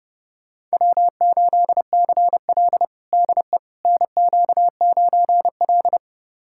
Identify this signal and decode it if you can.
Morse code, heard as a keyed tone, about 30 wpm, 700 Hz. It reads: W8CL DE NQ9L
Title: USA Callsign DE Callsign - 4 Characters - 30wpm